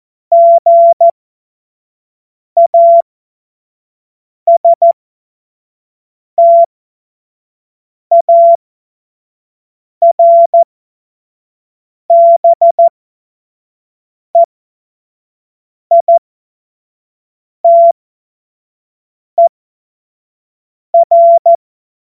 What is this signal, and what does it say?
Morse code, heard as a keyed tone, about 14 wpm, spaced as for 5 wpm, 680 Hz. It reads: GASTARBEITER
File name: Morsekurs / Witze / Witz 09. Dezember 2025 25 F70